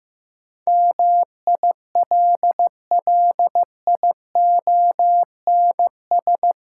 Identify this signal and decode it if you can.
Morse code, heard as a keyed tone, about 15 wpm, 700 Hz. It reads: MILLIONS
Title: Sets of 5 Words - New Word Drill - 15wpm